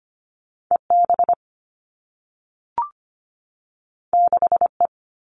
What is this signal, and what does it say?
Morse code, heard as a keyed tone, about 25 wpm, 700 Hz. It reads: E B 6 E
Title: All Cut Numbers - 2-Digit 25wpm